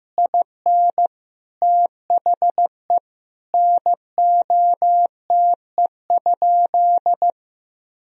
Morse code, heard as a keyed tone, about 15 wpm, 700 Hz. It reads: IN THE NOTE?
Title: Sentences from Top 1000 Words - Review Part 2 - 15wpm